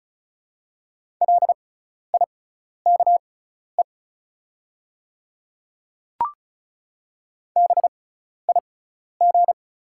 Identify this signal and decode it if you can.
Morse code, heard as a keyed tone, about 36 wpm, spaced as for 12 wpm, 690 Hz. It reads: LIKE BIG
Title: 6x Character Spacing - Top 300 Words - 35wpm